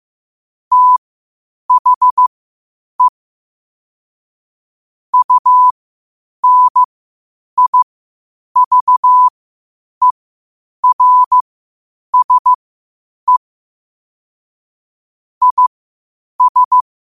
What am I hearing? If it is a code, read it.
Morse code, heard as a keyed tone, about 15 wpm, spaced as for 8 wpm, 1000 Hz. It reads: THE UNIVERSE IS
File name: qotdfiles / 8wpm